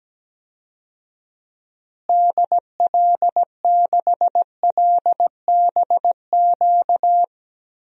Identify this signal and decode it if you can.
Morse code, heard as a keyed tone, about 17 wpm, 700 Hz. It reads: DL6LBQ
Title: DXCC-100 Call Signs - 6 Characters - Repeated 3x - 17wpm